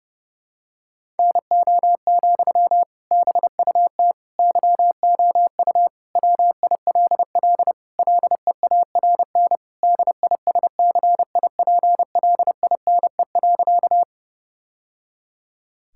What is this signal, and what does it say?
Morse code, heard as a keyed tone, about 29 wpm, 700 Hz. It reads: NO, BUT YOU WILL LEARN DISCIPLINE.
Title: Sentences from Top 1000 Words - Review Part 6 - 30wpm